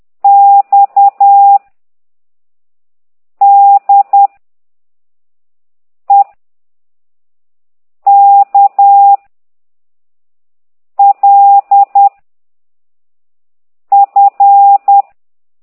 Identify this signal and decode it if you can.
Morse code, heard as a keyed tone, about 10 wpm, 800 Hz. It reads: X D E K L F
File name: ct2idw / exerc9